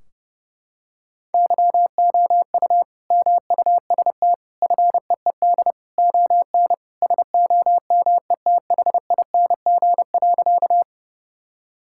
Morse code, heard as a keyed tone, about 31 wpm, 700 Hz. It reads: YOU MUST FEED ON SOMETHING.
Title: Sentences from Top 1600 Words 30wpm